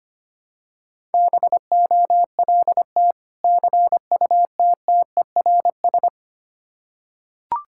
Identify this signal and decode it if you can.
Morse code, heard as a keyed tone, about 25 wpm, 700 Hz. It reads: BOLT CUTTERS
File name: practice / 25wpm / Difficult Words with Hints - 25wpm